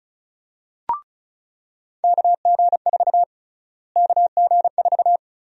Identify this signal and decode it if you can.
Morse code, heard as a keyed tone, about 34 wpm, 700 Hz. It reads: KG4 KG4
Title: USA Call Sign Prefixes - Repeated 2x - 35wpm